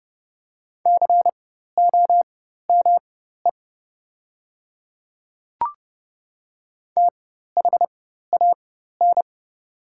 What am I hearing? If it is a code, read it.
Morse code, heard as a keyed tone, about 31 wpm, spaced as for 14 wpm, 700 Hz. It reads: COME THAN
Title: 4x Character Spacing - Top 100 Words - 30wpm